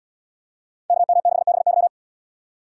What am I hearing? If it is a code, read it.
Morse code, heard as a keyed tone, about 75 wpm, 700 Hz. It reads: KM6BY